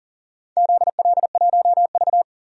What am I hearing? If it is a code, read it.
Morse code, heard as a keyed tone, about 40 wpm, 700 Hz. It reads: ZL1V
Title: DXCC-100 Call Signs - 4 Characters - 40wpm